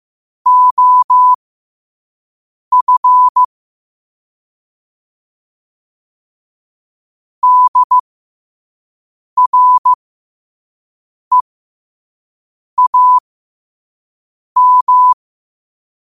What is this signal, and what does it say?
Morse code, heard as a keyed tone, about 15 wpm, spaced as for 5 wpm, 1000 Hz. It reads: OF DREAM
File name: qotdfiles / 5wpm